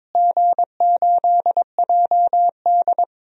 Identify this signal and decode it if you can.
Morse code, heard as a keyed tone, about 22 wpm, 700 Hz. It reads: G8JD
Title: DXCC-100 Call Signs - 4 Characters - Repeated 2x - 22wpm